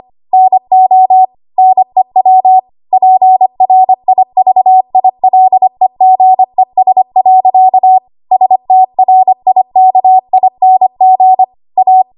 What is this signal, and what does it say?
Morse code, heard as a keyed tone, about 25 wpm, 750 Hz. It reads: NO NEW PRIVILEGES. STRIKING A
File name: Archive / 25 WPM / 25WPM